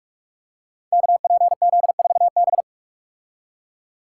Morse code, heard as a keyed tone, about 44 wpm, 700 Hz. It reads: KPZ4B
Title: Code Groups - 5 Letter:Numbers - 45wpm